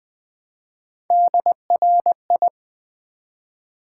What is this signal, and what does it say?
Morse code, heard as a keyed tone, about 20 wpm, 700 Hz. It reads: DRI
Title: Build A Word - 1 Letter At A Time - 5-Letter Words - 20wpm